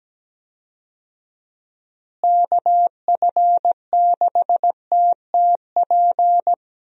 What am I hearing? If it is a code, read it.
Morse code, heard as a keyed tone, about 17 wpm, 700 Hz. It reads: KF6TTP
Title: USA Call Signs - 6 Characters - Repeated 3x - 17wpm